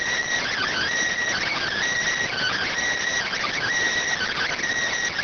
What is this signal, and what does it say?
Emergency Alert System (SAME) tones, heard as harsh buzzing bursts. It is heard from 0.5 to 4.5 s.